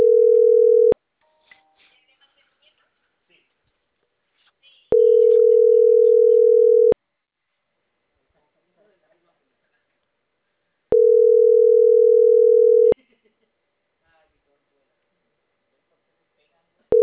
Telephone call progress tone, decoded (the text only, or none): ringback tone